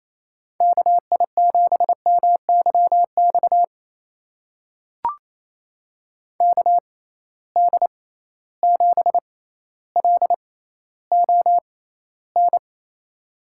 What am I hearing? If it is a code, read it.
Morse code, heard as a keyed tone, about 28 wpm, spaced as for 14 wpm, 700 Hz. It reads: KI7MYX KD7LON
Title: USA Call Signs - 6 Characters - 6x Character Spacing - 28wpm